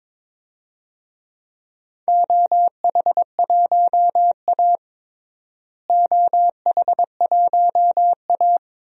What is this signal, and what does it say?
Morse code, heard as a keyed tone, about 21 wpm, 700 Hz. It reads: OH1A OH1A
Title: DXCC-100 Call Signs - 4 Characters - Repeated 3x - 22wpm